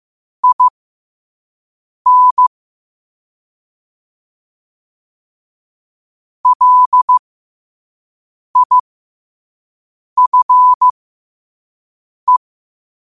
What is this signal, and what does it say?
Morse code, heard as a keyed tone, about 15 wpm, spaced as for 5 wpm, 1000 Hz. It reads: IN LIFE
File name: qotdfiles / 5wpm